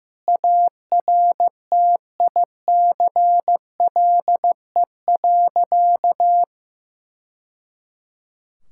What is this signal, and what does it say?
Morse code, heard as a keyed tone, about 15 wpm, 700 Hz. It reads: ARTICLE.
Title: Sentences from Top 1000 Words - Review Part 5 - 15wpm